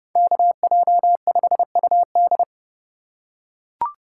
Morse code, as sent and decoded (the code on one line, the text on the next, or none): -.- .--- ..... ..- -..
KJ5UD